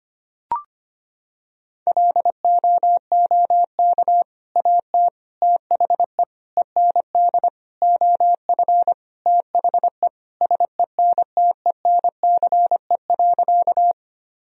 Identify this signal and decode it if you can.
Morse code, heard as a keyed tone, about 25 wpm, 700 Hz. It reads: LOOK AT THE END OF THE SENTENCE.
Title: Sentences from Top 1000 Words - Review Part 1 - 25wpm